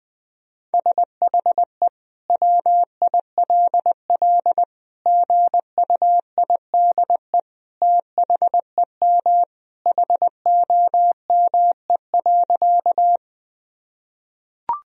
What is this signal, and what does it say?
Morse code, heard as a keyed tone, about 20 wpm, 700 Hz. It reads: SHE WILL GUIDE THEM HOME.
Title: Sentences from Top 1000 Words - Review Part 3 - 20wpm